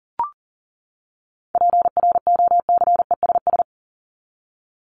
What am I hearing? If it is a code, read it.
Morse code, heard as a keyed tone, about 38 wpm, 700 Hz. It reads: PROCESS
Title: ES:S Suffix - Top 100 Words with Base Words 40wpm